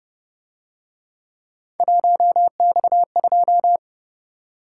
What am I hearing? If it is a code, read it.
Morse code, heard as a keyed tone, about 29 wpm, 700 Hz. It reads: 1X2
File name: Speed-Racing - Three Letter-Number Combinations - 25wpm